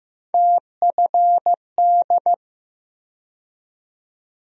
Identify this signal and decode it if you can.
Morse code, heard as a keyed tone, about 15 wpm, 700 Hz. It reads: TFD
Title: USA Call Sign Postfixes - 3 Letters 15wpm